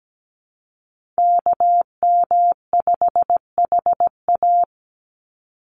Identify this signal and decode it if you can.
Morse code, heard as a keyed tone, about 17 wpm, 700 Hz. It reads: KM5HA
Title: USA Call Signs 17wpm